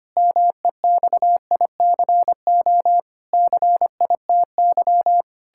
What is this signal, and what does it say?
Morse code, heard as a keyed tone, about 25 wpm, 700 Hz. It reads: MEXICO CITY
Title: Pangram Challenge v2 - 25wpm